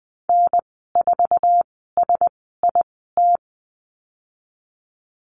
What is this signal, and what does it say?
Morse code, heard as a keyed tone, about 20 wpm, spaced as for 14 wpm, 700 Hz. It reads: N4SIT
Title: 055 Lesson 10g - Letter r - Call Signs 20wpm